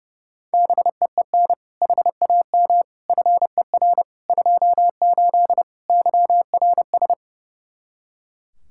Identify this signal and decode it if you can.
Morse code, heard as a keyed tone, about 30 wpm, 700 Hz. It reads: BEEN HAM FER 28 YRS
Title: K1USN SST QSO Practice 30wpm